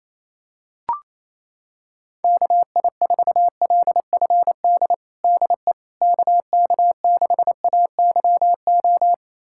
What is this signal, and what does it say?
Morse code, heard as a keyed tone, about 28 wpm, 700 Hz. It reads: KI4LFD DE KK6AYO
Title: USA Callsign DE Callsign 28wpm